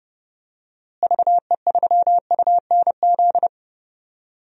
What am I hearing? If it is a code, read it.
Morse code, heard as a keyed tone, about 32 wpm, 700 Hz. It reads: VE3UNZ
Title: DXCC-100 Call Signs - 6 Characters - Repeated 2x - 30wpm